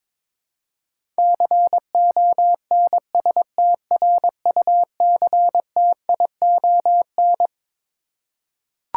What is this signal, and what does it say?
Morse code, heard as a keyed tone, about 22 wpm, 700 Hz. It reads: CONSTRUCTION
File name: Twelve Letter Words - 22wpm